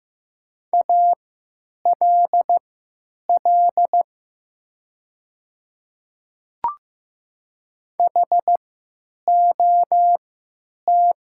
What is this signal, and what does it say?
Morse code, heard as a keyed tone, about 15 wpm, spaced as for 9 wpm, 700 Hz. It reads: ALL HOT
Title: 3x Character Spacing - Top 200 Words - 15wpm